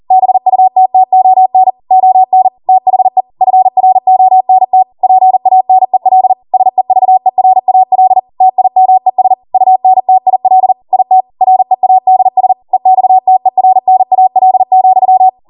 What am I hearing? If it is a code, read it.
Morse code, heard as a keyed tone, about 40 wpm, 750 Hz. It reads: BUTTON ON THE FRONT PANEL SEVERAL TIMES UNTIL IT READS EXTERNAL,